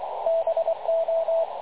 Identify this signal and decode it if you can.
Morse code, heard as a keyed tone, about 23 wpm, 660 Hz, with radio noise. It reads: BO
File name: dlazaawansowanych / nagraniazpasma / websdr recording 3567.0kHz